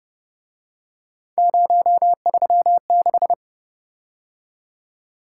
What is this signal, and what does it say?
Morse code, heard as a keyed tone, about 31 wpm, 700 Hz. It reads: 036
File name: Speed-Racing - Three Letter-Number Combinations - 25wpm